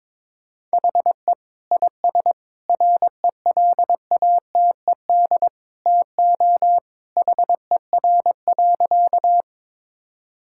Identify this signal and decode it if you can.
Morse code, heard as a keyed tone, about 22 wpm, 700 Hz. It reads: HE IS RELATED TO HER.
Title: Sentences from Top 1000 Words - Review Part 5 - 22wpm